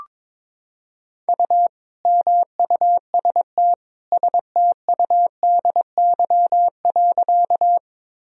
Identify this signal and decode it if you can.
Morse code, heard as a keyed tone, about 22 wpm, 700 Hz. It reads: U MUST STUDY.
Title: Sentences from Top 300 Words and Abbreviations 22wpm